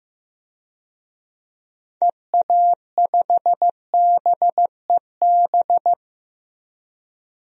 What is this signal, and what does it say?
Morse code, heard as a keyed tone, about 15 wpm, 700 Hz. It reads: EA5BEB
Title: DXCC-100 Call Signs - 6 Characters - 15wpm